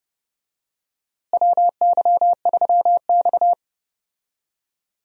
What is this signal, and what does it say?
Morse code, heard as a keyed tone, about 30 wpm, 700 Hz. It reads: WY3X
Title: USA Call Signs - 4 Characters - Repeated 2x - 30wpm